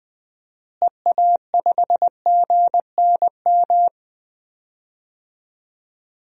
Morse code, heard as a keyed tone, about 20 wpm, 700 Hz. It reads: EA5GNM
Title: DXCC-100 Call Signs - 6 Characters - Repeated 2x - 20wpm